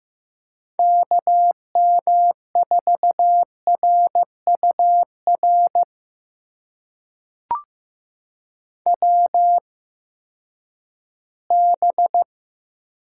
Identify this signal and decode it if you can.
Morse code, heard as a keyed tone, about 15 wpm, spaced as for 10 wpm, 700 Hz. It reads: KM4RUR WB
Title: USA Call Signs - 6 Characters - 8x Character Spacing - 15wpm